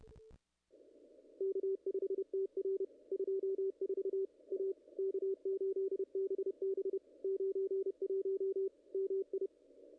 Morse code, heard as a keyed tone, about 27 wpm, 410 Hz, with radio noise. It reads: K5TR 24 A K8BB 91 MI
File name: contests / ss-cw-k5tr-op-n5rz -0100